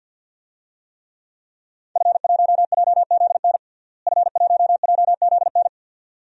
Morse code, heard as a keyed tone, about 46 wpm, 700 Hz. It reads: F1JZN F1JZN
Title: DXCC-100 Call Signs - 5 Characters - Repeated 2x - 50wpm